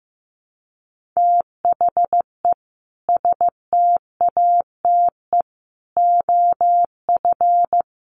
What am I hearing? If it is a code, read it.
Morse code, heard as a keyed tone, about 14 wpm, 700 Hz. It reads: THE STATE OF
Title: Sets of 5 Words Encore 15wpm